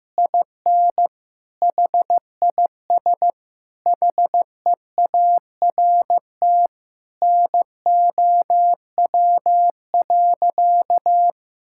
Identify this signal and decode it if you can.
Morse code, heard as a keyed tone, about 16 wpm, 700 Hz. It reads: IN HIS HEART NOW.
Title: Sentences from Top 1600 Words 15wpm